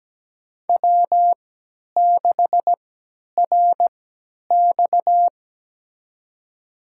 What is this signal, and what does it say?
Morse code, heard as a keyed tone, about 17 wpm, spaced as for 10 wpm, 700 Hz. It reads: W6RX
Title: USA Call Signs - 3x Character Spacing - 17wpm